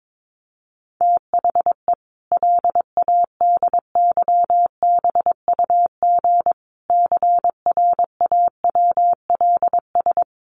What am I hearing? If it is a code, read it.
Morse code, heard as a keyed tone, about 22 wpm, 700 Hz. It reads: THE LADYBUG CRAWLS
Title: Additional Easy Phrases 22wpm